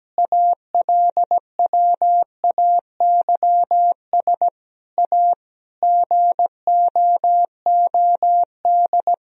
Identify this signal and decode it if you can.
Morse code, heard as a keyed tone, about 17 wpm, 700 Hz. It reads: ALWAYS A GOOD
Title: Sentences from Top 1000 Words - Review Part 4 - 17wpm